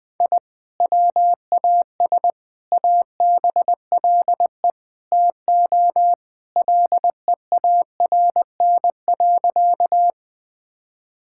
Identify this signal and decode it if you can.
Morse code, heard as a keyed tone, about 20 wpm, 700 Hz. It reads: I WAS ABLE TO LEARN.
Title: Sentences with 4-Word Phrases - 20wpm